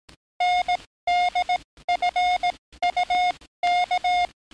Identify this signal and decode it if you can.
Morse code, heard as a keyed tone, about 18 wpm, spaced as for 14 wpm, 710 Hz. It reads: NDFUK
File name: training / NAVPERS 11049to11060 / NAVPERS 11057 RA RB